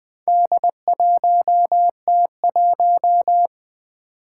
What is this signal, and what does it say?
Morse code, heard as a keyed tone, about 20 wpm, 700 Hz. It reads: D1T1
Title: Four Challenging Letter-Number Combinations 20wpm